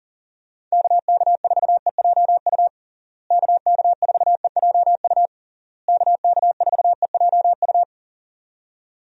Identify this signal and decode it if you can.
Morse code, heard as a keyed tone, about 40 wpm, 700 Hz. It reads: KK4EJU KK4EJU KK4EJU
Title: USA Call Signs - 6 Characters - Repeated 3x - 40wpm